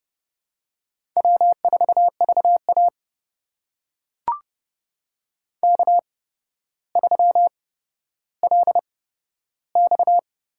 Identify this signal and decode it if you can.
Morse code, heard as a keyed tone, about 30 wpm, spaced as for 13 wpm, 700 Hz. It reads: W4VA K3LX
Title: USA Call Signs - 4 Characters - 8x Character Spacing - 30wpm